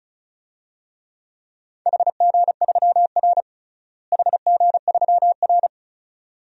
Morse code, heard as a keyed tone, about 36 wpm, 700 Hz. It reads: HG3R HG3R
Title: DXCC-100 Call Signs - 4 Characters - Repeated 2x - 35wpm